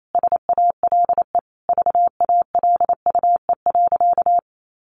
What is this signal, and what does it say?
Morse code, heard as a keyed tone, about 28 wpm, 700 Hz. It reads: SALE VALUE.
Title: Sentences from Top 1400 Words 28wpm